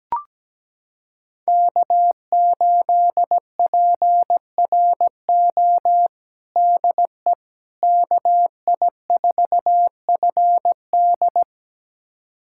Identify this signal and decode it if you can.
Morse code, heard as a keyed tone, about 17 wpm, 700 Hz. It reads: K8PRO DE KI4FD
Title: USA Callsign DE Callsign - 5 Characters - 17wpm